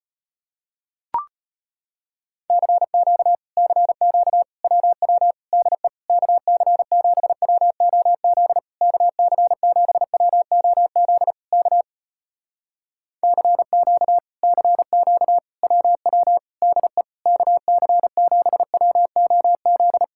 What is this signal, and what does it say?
Morse code, heard as a keyed tone, about 35 wpm, 700 Hz. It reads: CQ CQ WW DE KC7WOZ KC7WOZ K CQ CQ WW DE KC7WOZ